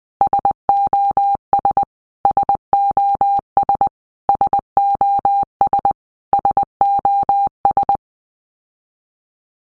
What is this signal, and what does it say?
Morse code, heard as a keyed tone, about 20 wpm, 800 Hz. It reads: SOS SOS SOS SOS
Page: SOS sound in Morse Code (Medium)